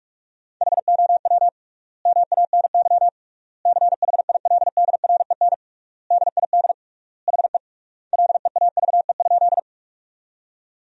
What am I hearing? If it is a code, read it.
Morse code, heard as a keyed tone, about 41 wpm, 700 Hz. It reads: HOW MANY CHILDREN DID HE LEAVE?
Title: 3x Word-Spacing - Sentences from Top 400 Words - 45wpm